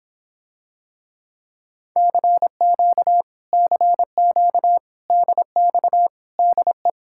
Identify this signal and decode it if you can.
Morse code, heard as a keyed tone, about 26 wpm, 700 Hz. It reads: CQ CQ DX DE